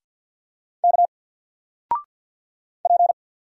Morse code, heard as a keyed tone, about 50 wpm, 700 Hz. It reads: K P
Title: Single Letter-Number 50wpm